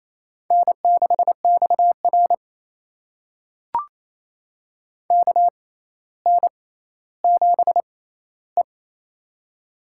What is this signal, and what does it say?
Morse code, heard as a keyed tone, about 28 wpm, spaced as for 14 wpm, 700 Hz. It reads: N6XR KN7E
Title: USA Call Signs - 4 Characters - 6x Character Spacing - 28wpm